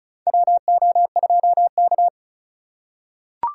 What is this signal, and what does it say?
Morse code, heard as a keyed tone, about 36 wpm, 700 Hz. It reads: WO2K
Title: USA Call Signs - 4x Character Spacing - 35wpm